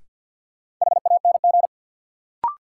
Morse code, heard as a keyed tone, about 52 wpm, 700 Hz. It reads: HANG
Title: Sentences from Top 1600 Words - New Word Drill Alt - 50wpm